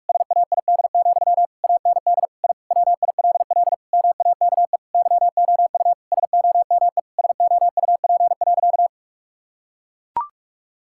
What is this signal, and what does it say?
Morse code, heard as a keyed tone, about 41 wpm, 700 Hz. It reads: SAID, AND I WILL MAKE YOU SOME SOUP.
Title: Sentences from Top 1000 Words - Review Part 6 - 45wpm